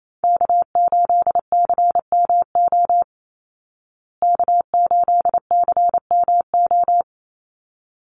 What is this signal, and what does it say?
Morse code, heard as a keyed tone, about 28 wpm, 700 Hz. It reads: K8CMO K8CMO
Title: Call Signs Repeated 3x - 28wpm